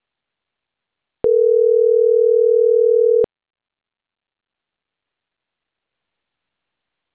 Ringback tone (North America), 440 + 480 Hz.
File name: CENTRAL CAEA / out agcaea06